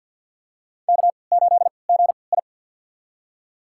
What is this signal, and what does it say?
Morse code, heard as a keyed tone, about 47 wpm, spaced as for 28 wpm, 700 Hz. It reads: K8GI
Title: USA Call Signs - 4 Characters - 3x Character Spacing - 50wpm